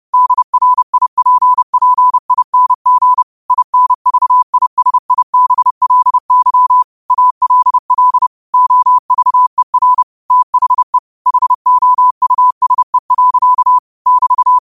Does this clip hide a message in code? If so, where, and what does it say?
Yes: Morse code from 0.1 to 14.6 s, heard as a keyed tone, about 30 wpm, 1000 Hz. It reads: DRIPPING INVISIBLY ALL OVER THE HOUSE. <BT>